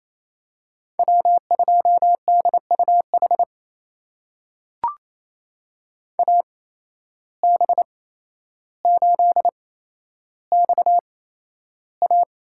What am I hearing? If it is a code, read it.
Morse code, heard as a keyed tone, about 28 wpm, spaced as for 12 wpm, 700 Hz. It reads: W2DUH AB8XA
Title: USA Call Signs - 5 Characters - 8x Character Spacing - 28wpm